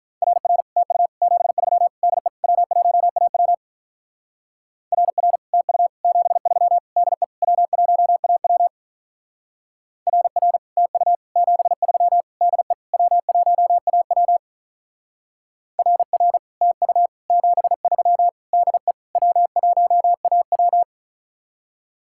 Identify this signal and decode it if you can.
Morse code, heard as a keyed tone, about 44 wpm, 700 Hz. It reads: RR TU 73 DE W1AW RR TU 73 DE W1AW RR TU 73 DE W1AW RR TU 73 DE W1AW